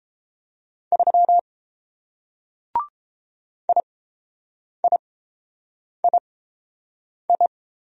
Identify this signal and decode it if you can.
Morse code, heard as a keyed tone, about 30 wpm, spaced as for 7 wpm, 700 Hz. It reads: 3 IIII